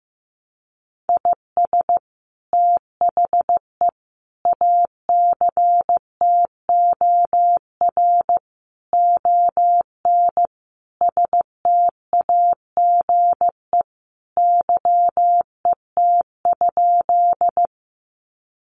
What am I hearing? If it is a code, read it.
Morse code, heard as a keyed tone, about 15 wpm, 700 Hz. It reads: IS THE ACTOR ON STAGE YET?
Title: Sentences from Top 1500 Words 15wpm